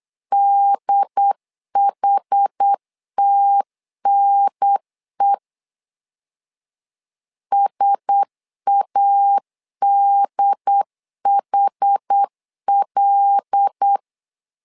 Morse code, heard as a keyed tone, about 8 wpm, 800 Hz. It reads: DHTNE SADHL
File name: LEZIONI / 5 LBVC / 5 L 8wpm 2831